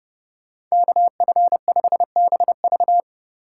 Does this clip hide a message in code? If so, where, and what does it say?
Yes: Morse code from 0.7 to 3.0 s, heard as a keyed tone, about 39 wpm, 700 Hz. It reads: KF5BV